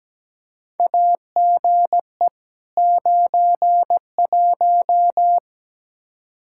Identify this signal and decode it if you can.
Morse code, heard as a keyed tone, about 17 wpm, 700 Hz. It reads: AGE 91
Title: Practice QSO - Exchange 3 - More Info - 17wpm